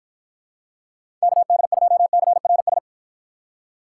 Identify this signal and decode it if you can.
Morse code, heard as a keyed tone, about 50 wpm, 700 Hz. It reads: KD2CRS